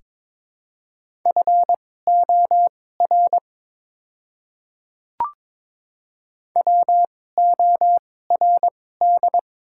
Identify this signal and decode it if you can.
Morse code, heard as a keyed tone, about 23 wpm, spaced as for 16 wpm, 700 Hz. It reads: FOR WORD
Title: 2x Character Spacing - Top 100 Words - 22wpm